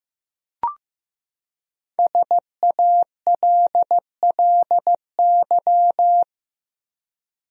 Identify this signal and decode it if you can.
Morse code, heard as a keyed tone, about 15 wpm, 700 Hz. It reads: SALLY
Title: USA Common Names - Max 5 Letters - 15wpm